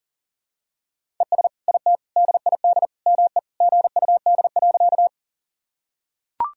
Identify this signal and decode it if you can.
Morse code, heard as a keyed tone, about 37 wpm, 700 Hz. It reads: ES IT DID ME GUD.
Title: Sentences from Top 200 Words and Abbreviations 40wpm